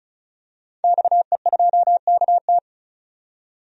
Morse code, heard as a keyed tone, about 34 wpm, 700 Hz. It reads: XE2KT
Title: DXCC-100 Call Signs - 5 Characters - 35wpm